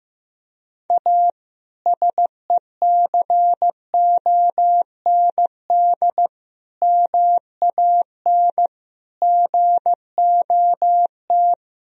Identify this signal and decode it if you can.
Morse code, heard as a keyed tone, about 15 wpm, 700 Hz. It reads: A SECOND MAN GOT
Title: Sentences from Top 300 Words and Abbreviations 15wpm